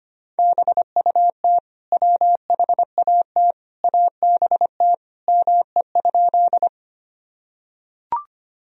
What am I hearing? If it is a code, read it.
Morse code, heard as a keyed tone, about 25 wpm, 700 Hz. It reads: BUT WHAT ABT ME?
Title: Sentences from Top 200 Words and Abbreviations 25wpm